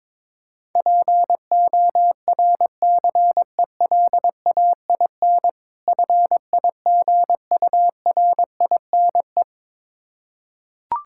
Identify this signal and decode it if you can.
Morse code, heard as a keyed tone, about 22 wpm, 690 Hz. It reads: PORCELAIN FIGURINE
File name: Difficult Words with Hints - 22wpm